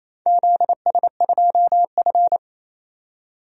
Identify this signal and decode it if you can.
Morse code, heard as a keyed tone, about 28 wpm, 700 Hz. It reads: ZS2F
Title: DXCC-100 Call Signs - 4 Characters - 28wpm